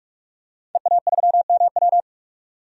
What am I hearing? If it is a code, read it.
Morse code, heard as a keyed tone, about 46 wpm, 700 Hz. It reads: EA3MW
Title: DXCC-100 Call Signs - 5 Characters - 45wpm